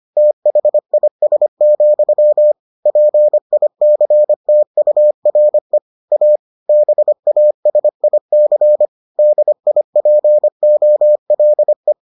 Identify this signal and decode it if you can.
Morse code, heard as a keyed tone, about 25 wpm, 600 Hz. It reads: THIS, PICTURE A BASIC DIPOLE